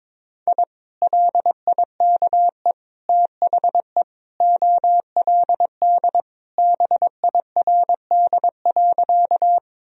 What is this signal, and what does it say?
Morse code, heard as a keyed tone, about 21 wpm, 700 Hz. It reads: I LIKE THE OLD BIRD.
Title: Sentences from Top 1000 Words - Review Part 2 - 22wpm